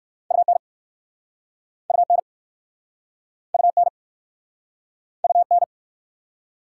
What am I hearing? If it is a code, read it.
Morse code, heard as a keyed tone, about 60 wpm, 690 Hz. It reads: UN UN UN UN